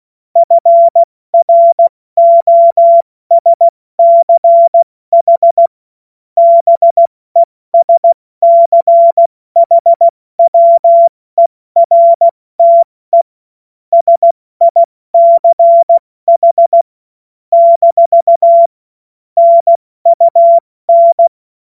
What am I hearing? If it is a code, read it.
Morse code, heard as a keyed tone, about 16 wpm, spaced as for 14 wpm, 680 Hz. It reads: FROSCH BESCHWERTE SICH - NUN
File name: Morsekurs / Witze / Witz 08. November 2022 70 F80